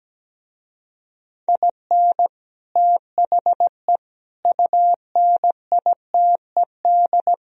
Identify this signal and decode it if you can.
Morse code, heard as a keyed tone, about 17 wpm, 700 Hz. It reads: IN THE UNITED
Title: Speed-Racing - Sets of 5 Words 17wpm